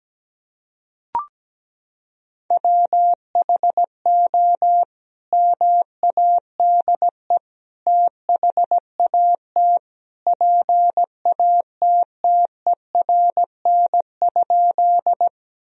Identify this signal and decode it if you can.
Morse code, heard as a keyed tone, about 17 wpm, 700 Hz. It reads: WHO MADE THAT PATTERN?
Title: Sentences from Top 1000 Words - Review Part 2 - 17wpm